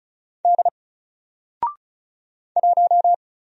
Morse code, heard as a keyed tone, about 36 wpm, 700 Hz. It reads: D 1
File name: Single-Letter-Number-35wpm